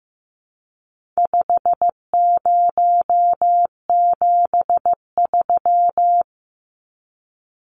Morse code, heard as a keyed tone, about 15 wpm, 700 Hz. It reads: 5073
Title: Numbers - 4-Digit 15wpm